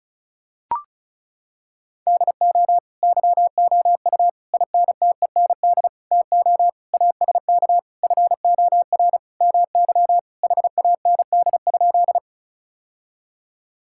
Morse code, heard as a keyed tone, about 36 wpm, 700 Hz. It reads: DO YOU INTEND TO ASK FOR MY HAND?
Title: Sentences from Top 1600 Words 35wpm